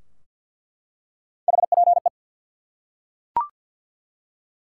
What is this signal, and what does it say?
Morse code, heard as a keyed tone, about 47 wpm, 700 Hz. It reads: HPE
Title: Top 90 QSO Elements 50wpm